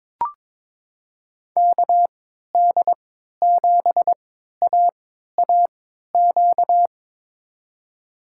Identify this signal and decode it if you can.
Morse code, heard as a keyed tone, about 22 wpm, spaced as for 13 wpm, 700 Hz. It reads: KD7AAQ
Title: USA Call Signs - 6 Characters - 3x Character Spacing - 22wpm